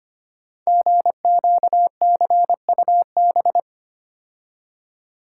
Morse code, heard as a keyed tone, about 25 wpm, 700 Hz. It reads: GQCUB
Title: Code Groups - 5 Letter:Numbers - 25wpm